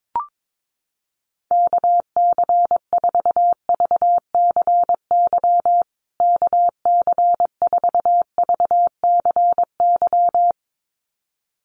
Morse code, heard as a keyed tone, about 22 wpm, 700 Hz. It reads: KC4VCY KC4VCY
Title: USA Call Signs Repeated 2x 22wpm